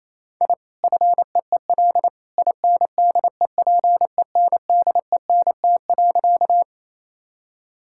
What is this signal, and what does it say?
Morse code, heard as a keyed tone, about 27 wpm, 700 Hz. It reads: I FEEL INDEPENDENT.
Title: Sentences from Top 1000 Words - Review Part 6 - 28wpm